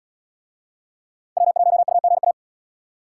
Morse code, heard as a keyed tone, about 77 wpm, 700 Hz. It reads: W1UGA